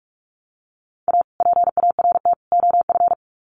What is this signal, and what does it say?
Morse code, heard as a keyed tone, about 44 wpm, 700 Hz. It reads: A PART OF
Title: Sets of 3 Words 45wpm